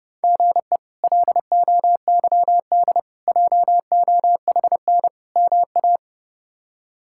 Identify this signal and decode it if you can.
Morse code, heard as a keyed tone, about 28 wpm, 700 Hz. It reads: GE LOYD JOHN MA
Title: K1USN SST QSO Practice - 30wpm